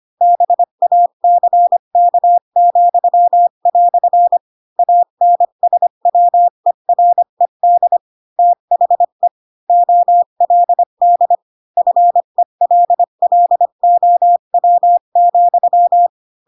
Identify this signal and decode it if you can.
Morse code, heard as a keyed tone, about 25 wpm, 700 Hz. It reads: BACK," ANSWERED THE OLD FELLOW,